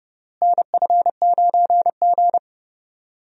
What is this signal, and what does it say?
Morse code, heard as a keyed tone, about 30 wpm, 700 Hz. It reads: NF9G
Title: USA Call Signs - 4 Characters - Repeated 3x - 30wpm